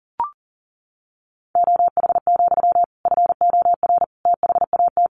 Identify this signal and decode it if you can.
Morse code, heard as a keyed tone, about 40 wpm, 700 Hz. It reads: OH, FOR THAT
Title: Sentences from Top 1400 Words 40wpm